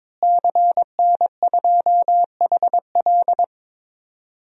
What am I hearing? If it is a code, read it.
Morse code, heard as a keyed tone, about 21 wpm, 700 Hz. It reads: CN2HL
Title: DXCC-100 Call Signs - 5 Characters - Repeated 3x - 22wpm